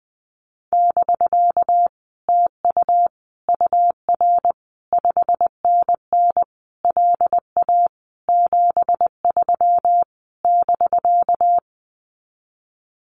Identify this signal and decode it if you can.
Morse code, heard as a keyed tone, about 20 wpm, 700 Hz. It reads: <BK> TU UR 5NN LA 73 <BK>
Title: Common QSO Phrases Encore 20wpm